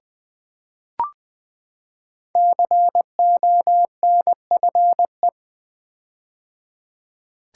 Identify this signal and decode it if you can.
Morse code, heard as a keyed tone, about 20 wpm, 700 Hz. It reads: CONFE
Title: Build A Word - 5 Letters At A Time - 10-Letter Words - 20wpm